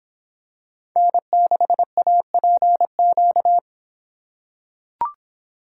Morse code, heard as a keyed tone, about 25 wpm, 700 Hz. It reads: N6APQ